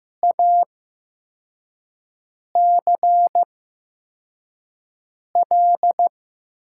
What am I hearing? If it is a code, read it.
Morse code, heard as a keyed tone, about 15 wpm, spaced as for 4 wpm, 700 Hz. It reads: ACL
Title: USA Call Signs - 5 Characters - 8x Character Spacing - 15wpm